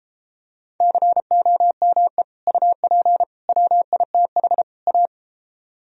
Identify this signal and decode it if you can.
Morse code, heard as a keyed tone, about 33 wpm, 700 Hz. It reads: COME UP WITH A